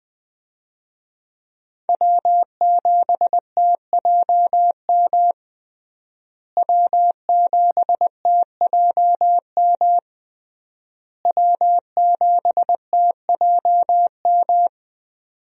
Morse code, heard as a keyed tone, about 21 wpm, 700 Hz. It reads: W7TJM W7TJM W7TJM
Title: USA Call Signs - 5 Characters - Repeated 3x - 20wpm